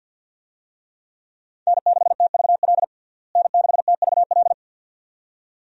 Morse code, heard as a keyed tone, about 51 wpm, 700 Hz. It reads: N6TVL N6TVL
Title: USA Call Signs - 5 Characters - Repeated 2x - 50wpm